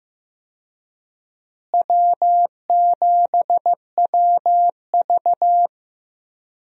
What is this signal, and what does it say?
Morse code, heard as a keyed tone, about 15 wpm, 700 Hz. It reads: W7WV
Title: USA Call Signs - 4 Characters - Repeated 3x - 15wpm